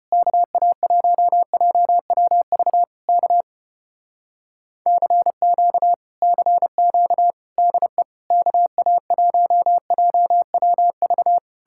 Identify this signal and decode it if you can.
Morse code, heard as a keyed tone, about 33 wpm, 700 Hz. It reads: KA1JWV K CQ CQ DE KA1JWV